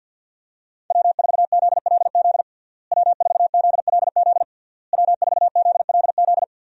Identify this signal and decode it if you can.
Morse code, heard as a keyed tone, about 48 wpm, 700 Hz. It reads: W4ZLZ W4ZLZ W4ZLZ
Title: USA Call Signs - 5 Characters - Repeated 3x - 50wpm